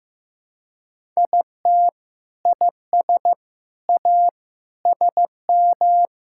Morse code, heard as a keyed tone, about 15 wpm, 700 Hz. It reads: IT IS A SM
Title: Sentences from Top 1000 Words - Review Part 5 - 15wpm